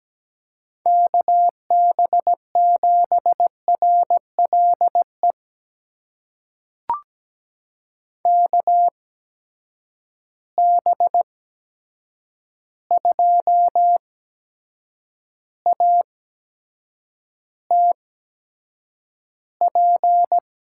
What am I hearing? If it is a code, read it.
Morse code, heard as a keyed tone, about 17 wpm, spaced as for 7 wpm, 700 Hz. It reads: KB7RLE KB2ATP
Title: USA Call Signs - 6 Characters - 8x Character Spacing - 17wpm